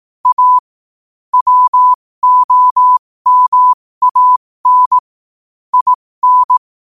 Morse code, heard as a keyed tone, about 18 wpm, spaced as for 15 wpm, 1000 Hz. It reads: A WOMAN IN
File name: qotdfiles / 15wpm